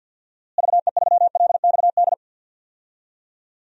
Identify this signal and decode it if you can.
Morse code, heard as a keyed tone, about 51 wpm, 700 Hz. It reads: VE3LXD
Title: DXCC-100 Call Signs - 6 Characters - Repeated 3x - 50wpm